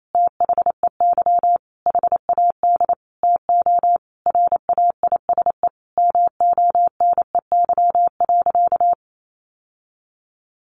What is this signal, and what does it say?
Morse code, heard as a keyed tone, about 28 wpm, 700 Hz. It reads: THEY HAD TO RAISE MONEY.
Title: Sentences from Top 1400 Words 28wpm